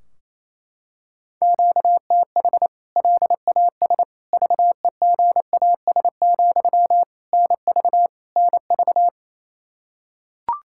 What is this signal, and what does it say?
Morse code, heard as a keyed tone, about 27 wpm, 700 Hz. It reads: QTH LAS VEGAS, NV NV
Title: K1USN SST QSO Practice 28wpm